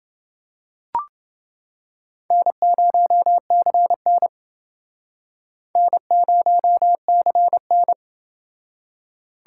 Morse code, heard as a keyed tone, about 29 wpm, 700 Hz. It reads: N0CN N0CN